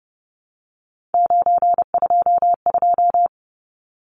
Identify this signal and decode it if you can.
Morse code, heard as a keyed tone, about 30 wpm, 700 Hz. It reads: 922
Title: Numbers - 3-Digit 30wpm